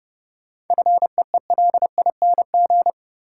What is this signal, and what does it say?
Morse code, heard as a keyed tone, about 30 wpm, 700 Hz. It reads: FEELING
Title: Sentences from Top 1200 Words - New Word Drill Alt - 30wpm